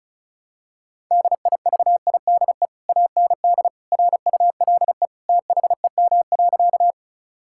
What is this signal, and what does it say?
Morse code, heard as a keyed tone, about 34 wpm, 700 Hz. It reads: DIVIDE AND RULE THEM.
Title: Sentences from Top 1600 Words 35wpm